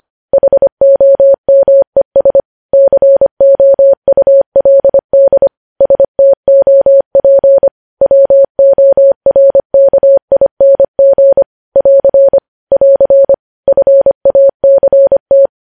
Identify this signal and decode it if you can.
Morse code, heard as a keyed tone, about 25 wpm, 550 Hz. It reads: HOMES COULD STOP WORKING <AR> <AR> FACT
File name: podcast / News headlines-1x-25x25